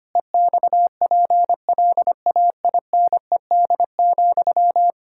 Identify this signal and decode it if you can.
Morse code, heard as a keyed tone, about 25 wpm, 700 Hz. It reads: EXPLAINED,
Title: Pangram Challenge v2 - 25wpm